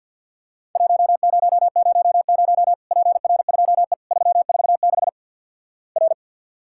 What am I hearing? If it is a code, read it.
Morse code, heard as a keyed tone, about 50 wpm, 700 Hz. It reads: 1000 PR2E 346 R
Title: Worked All Europe DX Contest - QTC Exchange with Repeats - 50wpm